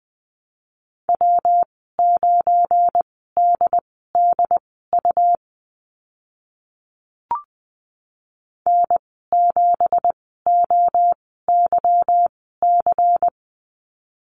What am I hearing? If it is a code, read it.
Morse code, heard as a keyed tone, about 20 wpm, spaced as for 15 wpm, 700 Hz. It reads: W9DDU N7OYC
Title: 186 Lesson 30g - Letter b - Call Signs 20wpm